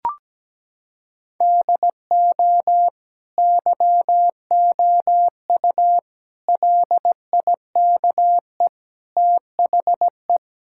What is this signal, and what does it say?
Morse code, heard as a keyed tone, about 17 wpm, 700 Hz. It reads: DO YOU LIKE THE
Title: Sentences from Top 1000 Words - Review Part 6 - 17wpm